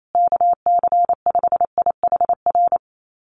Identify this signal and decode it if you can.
Morse code, heard as a keyed tone, about 28 wpm, 700 Hz. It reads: KC5IHR
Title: USA Call Signs Repeated 3x 28wpm